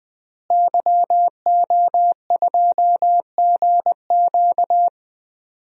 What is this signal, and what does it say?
Morse code, heard as a keyed tone, about 20 wpm, 700 Hz. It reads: YO2GQ
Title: DXCC-100 Call Signs - 5 Characters - Repeated 2x - 20wpm